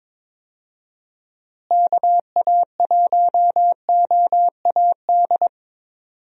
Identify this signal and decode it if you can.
Morse code, heard as a keyed tone, about 22 wpm, 700 Hz. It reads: KA1OAD
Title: USA Call Signs - 6 Characters - Repeated 3x - 22wpm